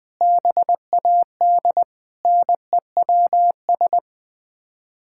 Morse code, heard as a keyed tone, about 21 wpm, 700 Hz. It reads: BAD NEWS
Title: Sets of 9 Words - 20wpm